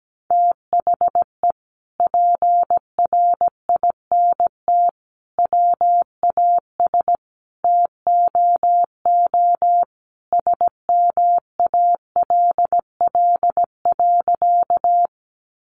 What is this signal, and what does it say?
Morse code, heard as a keyed tone, about 17 wpm, 700 Hz. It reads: THE PRINT WAS TOO SMALL.
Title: Sentences from Top 1400 Words 17wpm